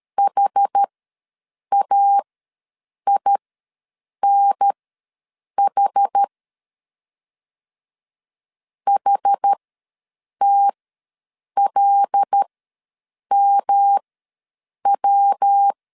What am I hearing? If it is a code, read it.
Morse code, heard as a keyed tone, about 13 wpm, spaced as for 7 wpm, 800 Hz. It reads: HAINH HTLMW
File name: LEZIONI / 5 LBVC / 5 L ws 12wpm 2958